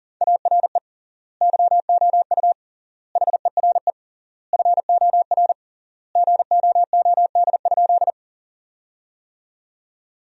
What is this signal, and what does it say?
Morse code, heard as a keyed tone, about 40 wpm, 700 Hz. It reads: ARE YOU HERE FOR GOOD?
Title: 3x Word-Spacing - Sentences from Top 200 Words - 40wpm